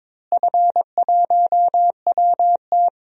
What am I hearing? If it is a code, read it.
Morse code, heard as a keyed tone, about 22 wpm, 700 Hz. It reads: F1WT
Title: DXCC-100 Call Signs - 4 Characters - Repeated 3x - 22wpm